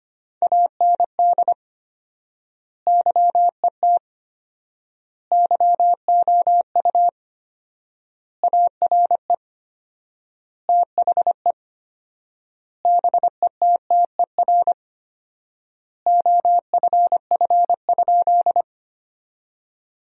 Morse code, heard as a keyed tone, about 25 wpm, 700 Hz. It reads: AND YET YOU ARE THE BETTER OFF?
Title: 4x Word-Spacing - Sentences from Top 500 Words - 25wpm